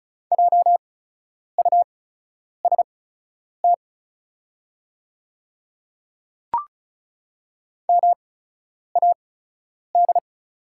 Morse code, heard as a keyed tone, about 34 wpm, spaced as for 10 wpm, 700 Hz. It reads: JUST MAD
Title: 8x Character Spacing - Top 300 Words - 35wpm